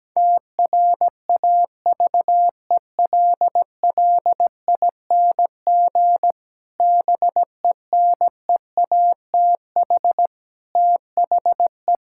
Morse code, heard as a keyed tone, about 17 wpm, 700 Hz. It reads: TRAVELLING BENEATH THE
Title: Pangram Challenge v2 - 17wpm